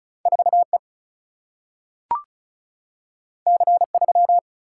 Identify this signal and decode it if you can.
Morse code, heard as a keyed tone, about 36 wpm, 700 Hz. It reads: 4 E C 3
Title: Single-Letter Single-Number 35wpm